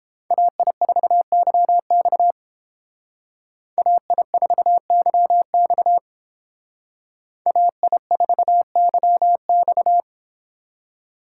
Morse code, heard as a keyed tone, about 30 wpm, 700 Hz. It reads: AI4YX AI4YX AI4YX